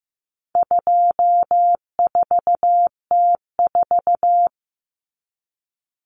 Morse code, heard as a keyed tone, about 15 wpm, 700 Hz. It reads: 24T4
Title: Common Cut Numbers - 4-Digit 15wpm